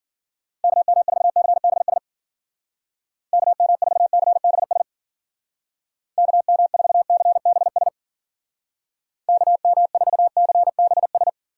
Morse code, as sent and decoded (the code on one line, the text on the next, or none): -.- -- ....- -.-. -... ... / -.- -- ....- -.-. -... ... / -.- -- ....- -.-. -... ... / -.- -- ....- -.-. -... ...
KM4CBS KM4CBS KM4CBS KM4CBS